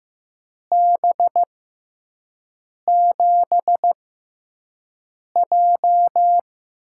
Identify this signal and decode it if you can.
Morse code, heard as a keyed tone, about 15 wpm, spaced as for 5 wpm, 700 Hz. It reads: B7J
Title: USA Call Signs - 6 Characters - 6x Character Spacing - 15wpm